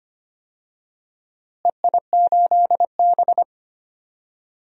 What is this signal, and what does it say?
Morse code, heard as a keyed tone, about 25 wpm, 700 Hz. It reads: EI8B
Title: DXCC-100 Call Signs - 4 Characters - Repeated 3x - 25wpm